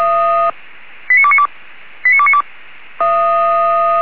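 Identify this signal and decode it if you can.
Five-tone selective call (ZVEI), heard as quick runs of five tones, tones of about 70 ms. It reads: 89292 89292
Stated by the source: Five-tone ZVEI selective call received with an SDR.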